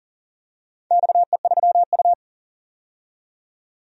Morse code, heard as a keyed tone, about 40 wpm, 700 Hz. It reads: XE3U
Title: DXCC-100 Call Signs - 4 Characters - 40wpm